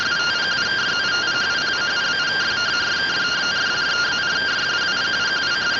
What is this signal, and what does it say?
Two-tone teleprinter signal, about 170 Hz shift.